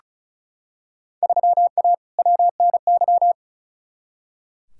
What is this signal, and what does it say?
Morse code, heard as a keyed tone, about 36 wpm, 700 Hz. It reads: 3A WNY
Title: Field Day Exchanges - 35wpm